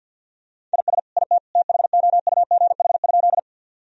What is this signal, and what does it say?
Morse code, heard as a keyed tone, about 49 wpm, 700 Hz. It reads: IS IT THOUGH?
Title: Sentences from Top 1000 Words - Review Part 2 - 50wpm